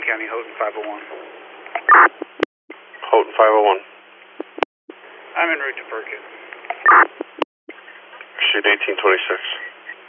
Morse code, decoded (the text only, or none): none